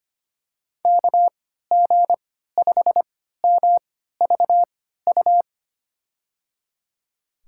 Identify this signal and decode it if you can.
Morse code, heard as a keyed tone, about 25 wpm, spaced as for 14 wpm, 700 Hz. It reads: KG5MVU
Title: USA Call Signs - 6 Characters - 3x Character Spacing - 25wpm